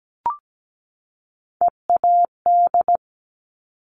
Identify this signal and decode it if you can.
Morse code, heard as a keyed tone, about 17 wpm, 700 Hz. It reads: EAD
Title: Top Three Letter Combinations in Words 17wpm